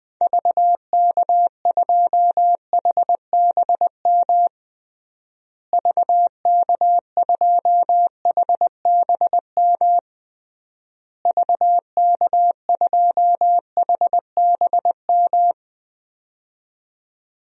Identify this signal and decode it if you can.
Morse code, heard as a keyed tone, about 20 wpm, 700 Hz. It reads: VK2HBM VK2HBM VK2HBM
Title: DXCC-100 Call Signs - 6 Characters - Repeated 3x - 20wpm